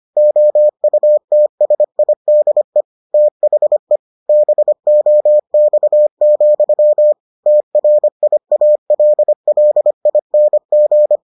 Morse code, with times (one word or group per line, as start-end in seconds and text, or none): 0.2-2.8 OUTSIDE
3.1-4.0 THE
4.3-7.1 BOX,
7.5-11.2 TRIALLING